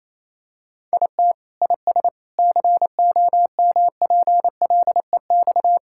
Morse code, heard as a keyed tone, about 27 wpm, 700 Hz. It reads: IT IS COMPLEX
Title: Sentences from Top 1000 Words - Review Part 6 - 28wpm